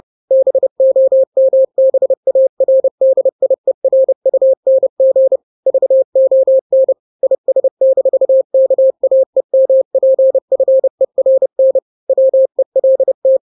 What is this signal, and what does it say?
Morse code, heard as a keyed tone, about 30 wpm, 530 Hz. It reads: BOMBARDIERUNG VON IS-KAEMPFERN WELT
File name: 29wpm